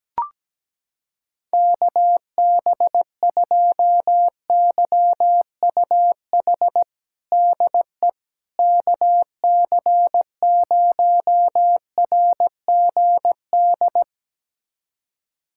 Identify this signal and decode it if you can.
Morse code, heard as a keyed tone, about 17 wpm, 700 Hz. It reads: KB2YUH DE KC0RGD
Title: USA Callsign DE Callsign - 6 Characters - 17wpm